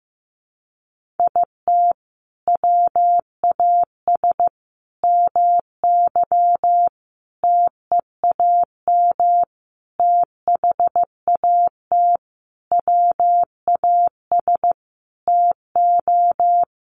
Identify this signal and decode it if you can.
Morse code, heard as a keyed tone, about 15 wpm, 700 Hz. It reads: IT WAS MY TEAM THAT WAS TO
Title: Sentences from Top 1500 Words 15wpm